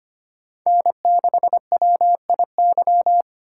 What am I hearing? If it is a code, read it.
Morse code, heard as a keyed tone, about 25 wpm, 700 Hz. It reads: N6WIY
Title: USA Call Signs - 5 Characters - 3x Character Spacing - 25wpm